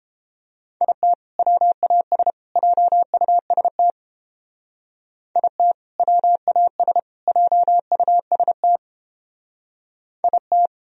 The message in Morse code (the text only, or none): IT WAS JUST IT WAS JUST IT